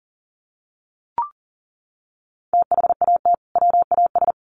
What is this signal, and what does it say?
Morse code, heard as a keyed tone, about 40 wpm, 700 Hz. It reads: THAT WAS
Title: Sets of 2 Words 40wpm